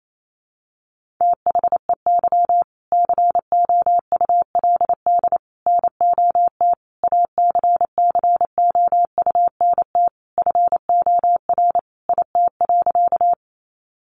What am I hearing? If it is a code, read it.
Morse code, heard as a keyed tone, about 28 wpm, 700 Hz. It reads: THEY COULD NOT ACCOUNT FOR IT.
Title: Sentences from Top 900 Words 28wpm